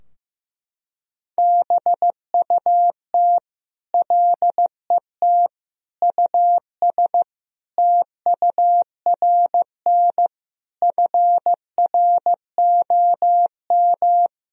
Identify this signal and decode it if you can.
Morse code, heard as a keyed tone, about 15 wpm, 700 Hz. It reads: BUT LET US TURN FROM
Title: Sentences from Top 1000 Words - Review Part 4 - 15wpm